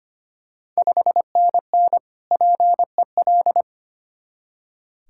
Morse code, heard as a keyed tone, about 25 wpm, 690 Hz. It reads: 5NN PEL
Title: Ontario QSO Party Exchange 25wpm